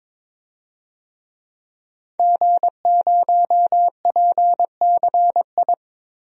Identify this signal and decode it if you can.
Morse code, heard as a keyed tone, about 22 wpm, 700 Hz. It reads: G0PCI
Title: DXCC-100 Call Signs - 5 Characters - Repeated 3x - 22wpm